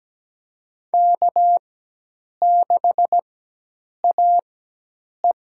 Morse code, heard as a keyed tone, about 17 wpm, spaced as for 8 wpm, 700 Hz. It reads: K6AE
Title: USA Call Signs - 4x Character Spacing - 17wpm